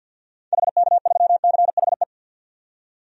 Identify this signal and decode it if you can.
Morse code, heard as a keyed tone, about 50 wpm, 700 Hz. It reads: HK3CHE
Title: DXCC-100 Call Signs - 6 Characters - 50wpm